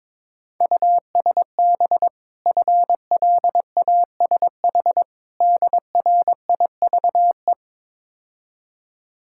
Morse code, heard as a keyed tone, about 22 wpm, 690 Hz. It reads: USB FLASH DRIVE
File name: Easy Words - Speed-Racing with Hints - 22wpm